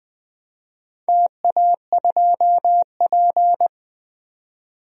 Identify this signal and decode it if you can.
Morse code, heard as a keyed tone, about 21 wpm, 700 Hz. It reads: TA2P
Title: DXCC-100 Call Signs - 4 Characters - 20wpm